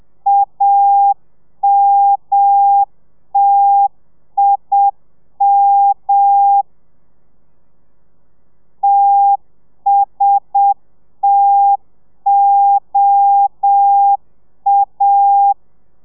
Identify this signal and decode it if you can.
Morse code, heard as a keyed tone, about 7 wpm, 800 Hz. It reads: AMTIM TSTOA